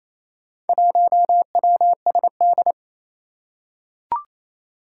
Morse code, heard as a keyed tone, about 27 wpm, 700 Hz. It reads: 1WSD
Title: Four Challenging Letter-Number Combinations 28wpm